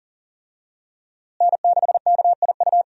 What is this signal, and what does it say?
Morse code, heard as a keyed tone, about 40 wpm, 700 Hz. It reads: N6KIU
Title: USA Call Signs - 5 Characters - Repeated 3x - 40wpm